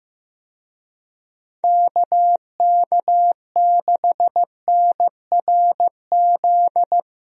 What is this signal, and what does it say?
Morse code, heard as a keyed tone, about 15 wpm, 700 Hz. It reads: KK6NRZ
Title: USA Call Signs - 6 Characters - Repeated 2x - 15wpm